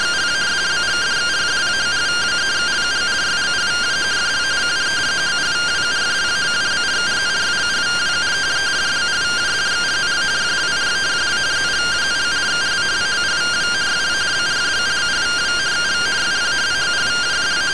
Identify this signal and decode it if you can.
Two-tone teleprinter signal, about 170 Hz shift.